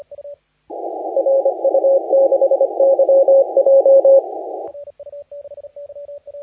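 Morse code, heard as a keyed tone, about 25 wpm, 560 Hz. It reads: RU6YJ
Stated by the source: Morse contest audio recorded by LA4C during the SAC CW 2018 contest.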